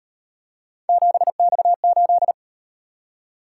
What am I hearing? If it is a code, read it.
Morse code, heard as a keyed tone, about 37 wpm, 700 Hz. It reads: 7X8